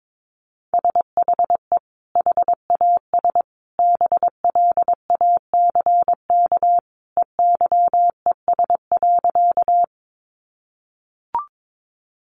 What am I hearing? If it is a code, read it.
Morse code, heard as a keyed tone, about 22 wpm, 700 Hz. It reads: SHE HAS BLACK EYES.
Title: Easy Phrases 22wpm